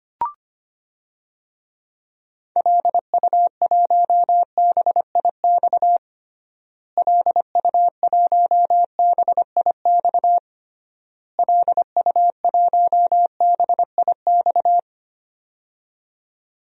Morse code, heard as a keyed tone, about 25 wpm, 700 Hz. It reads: LU1BIX LU1BIX LU1BIX
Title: DXCC-100 Call Signs - 6 Characters - Repeated 3x - 25wpm